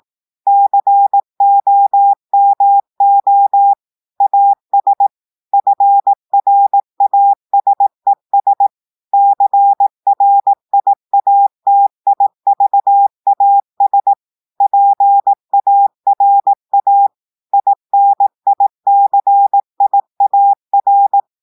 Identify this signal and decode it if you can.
Morse code, heard as a keyed tone, about 18 wpm, 800 Hz. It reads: COMO AS FRASES CRIATIVAS PARA INICIAR
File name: audios / portugais / portugais texte en clair 2 vitesse 18 mots minutes